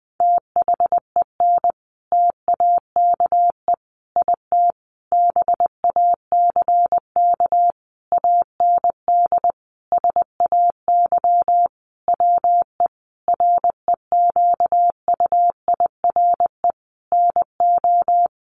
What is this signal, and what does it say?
Morse code, heard as a keyed tone, about 20 wpm, 700 Hz. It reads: THEN TAKE IT BACK AND SAY WE REQUIRE NO
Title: Sentences from Top 1500 Words 20wpm